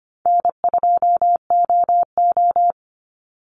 Morse code, heard as a keyed tone, about 25 wpm, 700 Hz. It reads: N2OO
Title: Call Signs - 25wpm